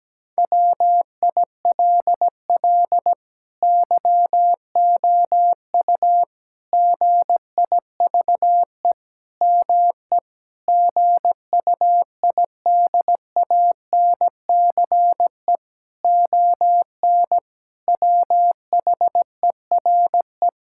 Morse code, heard as a keyed tone, about 17 wpm, 700 Hz. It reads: WILL YOU GIVE ME GUIDANCE ON WHERE
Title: Sentences from Top 1600 Words 17wpm